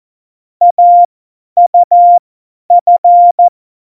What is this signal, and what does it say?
Morse code, heard as a keyed tone, about 14 wpm, spaced as for 10 wpm, 700 Hz. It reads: AUF
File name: Morsekurs / Witze / Witz 11 August 2020, 50 F70